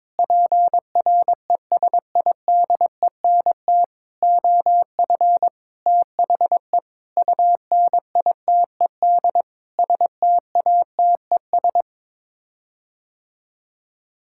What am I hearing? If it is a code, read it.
Morse code, heard as a keyed tone, about 21 wpm, 700 Hz. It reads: PRESIDENT OF THE UNITED STATES
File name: Sets of 9 Words - 22wpm